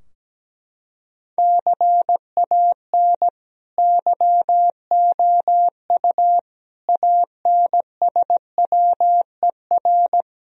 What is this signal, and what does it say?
Morse code, heard as a keyed tone, about 17 wpm, 700 Hz. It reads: CAN YOU ANSWER
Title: Sentences from Top 1000 Words - Review Part 3 - 17wpm